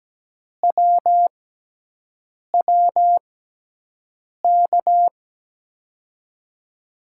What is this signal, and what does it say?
Morse code, heard as a keyed tone, about 17 wpm, spaced as for 6 wpm, 700 Hz. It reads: WWK
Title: USA Call Signs - 5 Characters - 6x Character Spacing - 17wpm